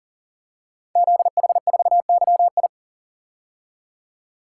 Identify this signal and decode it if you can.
Morse code, heard as a keyed tone, about 40 wpm, 690 Hz. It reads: ZH4YI
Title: Code Groups - 5 Letter:Numbers - 40wpm